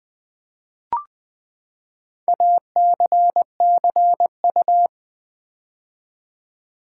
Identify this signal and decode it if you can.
Morse code, heard as a keyed tone, about 20 wpm, 700 Hz. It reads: ACCU
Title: Build A Word - 4 Letters At A Time - 10-Letter Words - 20wpm